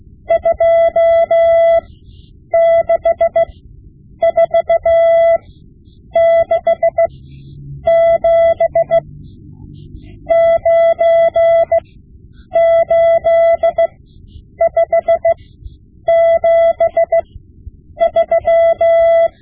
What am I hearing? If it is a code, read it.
Morse code, heard as a keyed tone, about 16 wpm, spaced as for 8 wpm, 670 Hz. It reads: 26467 98573
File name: Morse Lessons / Morse Lessons / Tape-5B